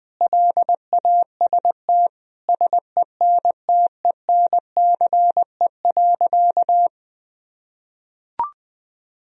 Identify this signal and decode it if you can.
Morse code, heard as a keyed tone, about 20 wpm, 700 Hz. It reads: LAST SENTENCE.
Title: Sentences from Top 1000 Words - Review Part 1 - 20wpm